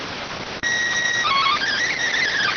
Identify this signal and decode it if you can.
SSTV (slow-scan television), PD 290.